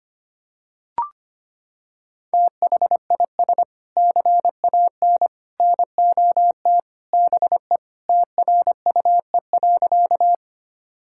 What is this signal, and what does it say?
Morse code, heard as a keyed tone, about 25 wpm, 700 Hz. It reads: THIS CAN NOT BE TRUE.
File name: practice / 25wpm / Sentences with 3-Word Phrases - 25wpm